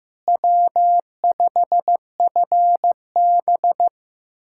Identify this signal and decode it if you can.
Morse code, heard as a keyed tone, about 15 wpm, 700 Hz. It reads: W5FB
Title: USA Callsign DE Callsign - 4 Characters - 15wpm